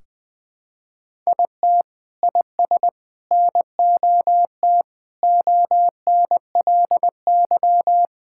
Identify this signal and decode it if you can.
Morse code, heard as a keyed tone, about 20 wpm, 700 Hz. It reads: IT IS NOT ONLY
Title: Sentences from Top 1600 Words 20wpm